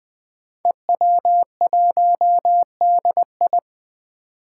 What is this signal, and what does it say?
Morse code, heard as a keyed tone, about 20 wpm, 700 Hz. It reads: EW1DI
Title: Five Challenging Letter-Number Combinations 20wpm